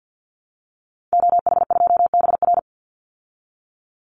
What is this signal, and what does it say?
Morse code, heard as a keyed tone, about 50 wpm, 700 Hz. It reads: OH2BR
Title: Call Signs Repeated 3x - 50wpm